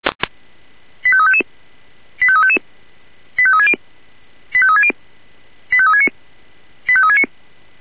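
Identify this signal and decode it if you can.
Five-tone selective call (ZVEI), heard as quick runs of five tones, tones of about 70 ms. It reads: 85355 85355 85360 85360 85369 85369
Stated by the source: ZVEI five-tone selective call received on SDR.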